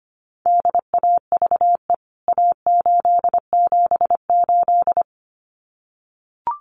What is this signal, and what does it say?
Morse code, heard as a keyed tone, about 25 wpm, 700 Hz. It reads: DAVE A878
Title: CWT Exchange 25wpm